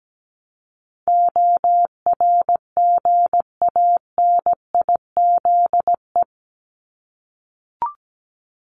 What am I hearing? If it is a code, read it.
Morse code, heard as a keyed tone, about 17 wpm, 700 Hz. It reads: ORGANIZE
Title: Top 1751-2000 Words 17wpm